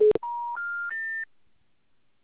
Special information tone.